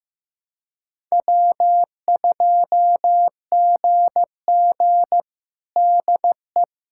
Morse code, heard as a keyed tone, about 15 wpm, 700 Hz. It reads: W2GG DE
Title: USA Callsign DE Callsign - 4 Characters - 15wpm